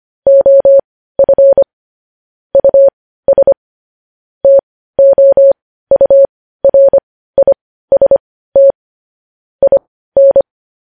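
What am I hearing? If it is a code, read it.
Morse code, heard as a keyed tone, about 25 wpm, spaced as for 15 wpm, 550 Hz. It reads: OF US TOURIST IN
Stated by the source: Morse news headlines sent at 15 words per minute.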